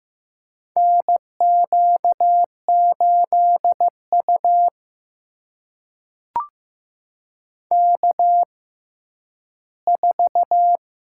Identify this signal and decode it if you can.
Morse code, heard as a keyed tone, about 15 wpm, spaced as for 10 wpm, 700 Hz. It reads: NQ8U K4
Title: USA Call Signs - 4 Characters - 6x Character Spacing - 15wpm